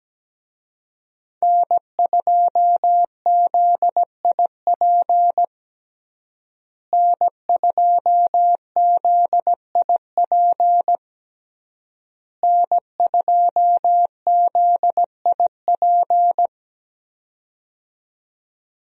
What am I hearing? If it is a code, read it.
Morse code, heard as a keyed tone, about 17 wpm, 700 Hz. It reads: N2ZIP N2ZIP N2ZIP
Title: USA Call Signs - 5 Characters - Repeated 3x - 17wpm